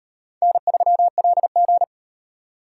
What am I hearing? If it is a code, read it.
Morse code, heard as a keyed tone, about 38 wpm, 700 Hz. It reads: N3LG